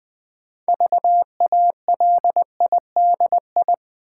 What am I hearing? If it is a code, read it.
Morse code, heard as a keyed tone, about 20 wpm, 700 Hz. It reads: VALIDI
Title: Build A Word - 2 Letters At A Time - 8-Letter Words - 20wpm